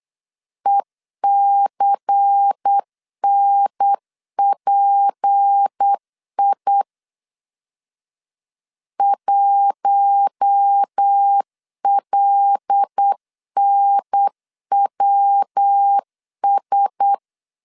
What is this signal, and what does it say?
Morse code, heard as a keyed tone, about 8 wpm, 800 Hz. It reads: ECNPI 1LNWS
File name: LEZIONI / 6 PKX0 / 6 PKX0 8wpm 3341